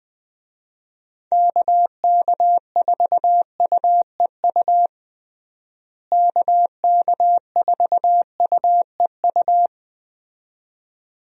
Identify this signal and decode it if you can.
Morse code, heard as a keyed tone, about 20 wpm, 700 Hz. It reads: KK4UEU KK4UEU
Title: USA Call Signs - 6 Characters - Repeated 2x - 20wpm